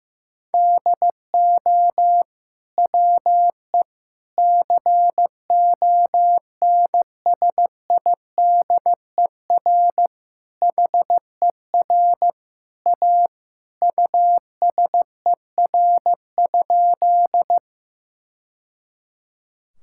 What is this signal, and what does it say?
Morse code, heard as a keyed tone, about 15 wpm, 700 Hz. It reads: DO WE CONSIDER HER A USER?
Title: Sentences from Top 1000 Words - Review Part 5 - 15wpm